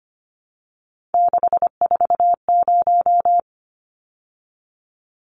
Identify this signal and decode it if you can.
Morse code, heard as a keyed tone, about 25 wpm, 690 Hz. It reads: 640
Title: Numbers - 3-Digit 25wpm